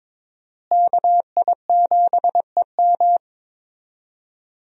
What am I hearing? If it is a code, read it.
Morse code, heard as a keyed tone, about 22 wpm, 700 Hz. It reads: KI7EM
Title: USA Call Signs - 5 Characters - Repeated 2x - 22wpm